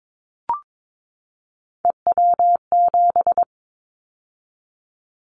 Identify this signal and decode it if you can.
Morse code, heard as a keyed tone, about 22 wpm, 700 Hz. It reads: EW7
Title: Callsign Prefixes for Top 100 DXCC - 22wpm